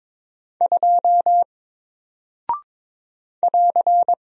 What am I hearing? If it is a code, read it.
Morse code, heard as a keyed tone, about 22 wpm, 700 Hz. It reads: 2 <AR>
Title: Single Character (Letter:Number:Punctuation:Prosign) Alt 22wpm